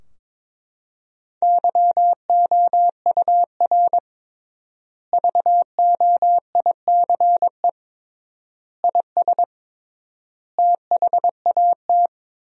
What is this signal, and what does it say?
Morse code, heard as a keyed tone, about 22 wpm, 700 Hz. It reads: YOUR VOICE IS THAT
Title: 3x Word-Spacing - Sentences from Top 500 Words - 22wpm